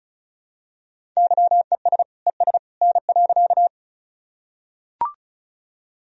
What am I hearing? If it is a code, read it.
Morse code, heard as a keyed tone, about 36 wpm, 700 Hz. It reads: YES ES N.
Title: Sentences from Top 500 Words and Abbreviations 35wpm